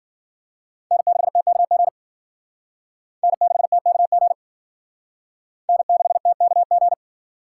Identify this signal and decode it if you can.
Morse code, heard as a keyed tone, about 54 wpm, 700 Hz. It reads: N6TKG N6TKG N6TKG